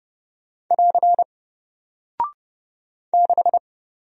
Morse code, heard as a keyed tone, about 30 wpm, 700 Hz. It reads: <AR> 6
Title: Single Character (Letter:Number:Punctuation:Prosign) Alt 30wpm